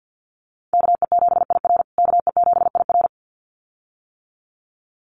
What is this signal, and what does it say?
Morse code, heard as a keyed tone, about 50 wpm, 700 Hz. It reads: KE7IR KE7IR
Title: USA Call Signs Repeated 2x 50wpm